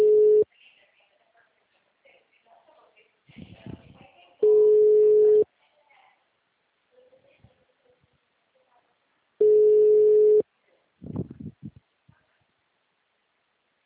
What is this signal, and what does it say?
Ringback tone (Europe, 425 Hz), with speech.